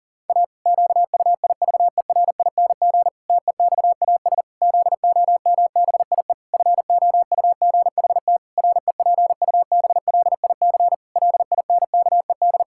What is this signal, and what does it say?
Morse code, heard as a keyed tone, about 40 wpm, 700 Hz. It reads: A QUIVERING TEXAS ZOMBIE FOUGHT REPUBLIC LINKED
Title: Pangram Challenge v2 - 40wpm